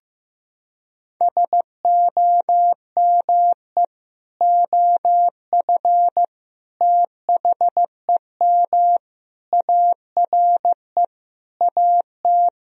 Morse code, heard as a keyed tone, about 16 wpm, 700 Hz. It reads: SOME OF THEM ARE AT
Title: Sentences from Top 1000 Words - Review Part 2 - 15wpm